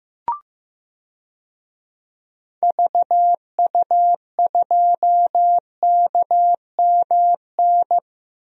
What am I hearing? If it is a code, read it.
Morse code, heard as a keyed tone, about 15 wpm, 700 Hz. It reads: VU2KMN
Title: DXCC-100 Call Signs - 6 Characters - Repeated 2x - 15wpm